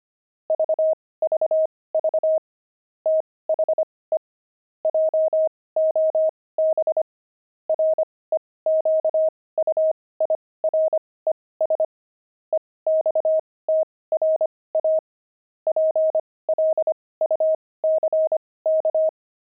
Morse code, heard as a keyed tone, about 25 wpm, spaced as for 18 wpm, 630 Hz. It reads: VVV THE JOB REQUIRES EXTRA PLUCK
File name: pangram-fs-18-cs-25-zeal